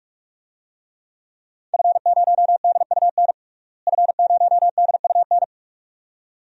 Morse code, heard as a keyed tone, about 46 wpm, 700 Hz. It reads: F0DUN F0DUN
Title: DXCC-100 Call Signs - 5 Characters - Repeated 2x - 45wpm